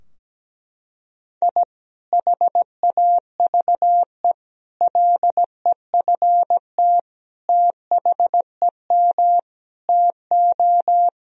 Morse code, heard as a keyed tone, about 17 wpm, 700 Hz. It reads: I HAVE LEFT THEM TO
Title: Sentences from Top 1000 Words - Review Part 2 - 17wpm